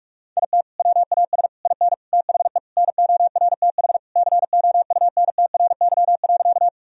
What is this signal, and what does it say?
Morse code, heard as a keyed tone, about 46 wpm, 700 Hz. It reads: IT WAS IN THE NORTH COUNTRY.
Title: Sentences from Top 1000 Words - Review Part 1 - 45wpm